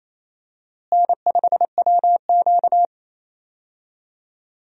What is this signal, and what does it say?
Morse code, heard as a keyed tone, about 28 wpm, 700 Hz. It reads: N5WQ
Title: USA Call Signs - 4 Characters - Repeated 2x - 28wpm